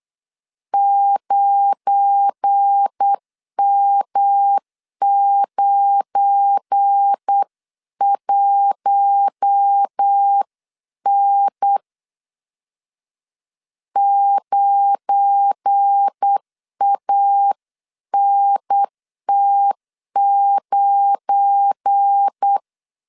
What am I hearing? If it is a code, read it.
Morse code, heard as a keyed tone, about 8 wpm, 800 Hz. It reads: 9M91N 9ANT9
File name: LEZIONI / 7 GF9Z / 7 GF9 8wpm 3545